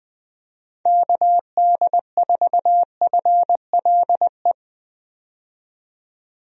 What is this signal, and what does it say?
Morse code, heard as a keyed tone, about 20 wpm, 700 Hz. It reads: KD4FLE
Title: USA Call Signs - 6 Characters - 20wpm